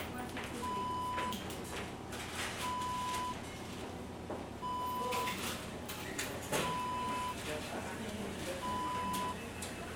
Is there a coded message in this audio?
No, only speech.